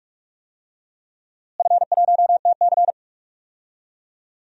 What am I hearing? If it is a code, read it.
Morse code, heard as a keyed tone, about 46 wpm, 700 Hz. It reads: F1TC
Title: DXCC-100 Call Signs - 4 Characters - Repeated 2x - 45wpm